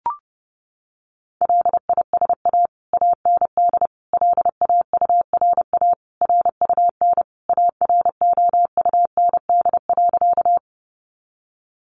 Morse code, heard as a keyed tone, about 28 wpm, 700 Hz. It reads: LISA AND LAURA RUN AROUND.
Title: Additional Easy Phrases 30wpm